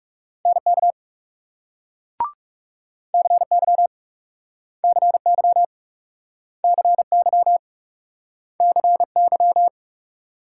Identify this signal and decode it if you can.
Morse code, heard as a keyed tone, about 38 wpm, 700 Hz. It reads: NK CY CY CY CY